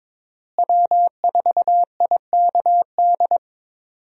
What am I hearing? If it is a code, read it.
Morse code, heard as a keyed tone, about 21 wpm, 700 Hz. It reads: W4IKD
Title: USA Call Signs - 5 Characters - Repeated 2x - 22wpm